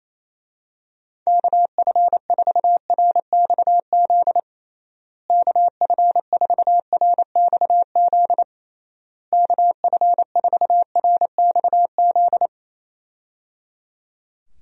Morse code, heard as a keyed tone, about 28 wpm, 700 Hz. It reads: KF4RXZ KF4RXZ KF4RXZ
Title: USA Call Signs - 6 Characters - Repeated 3x - 28wpm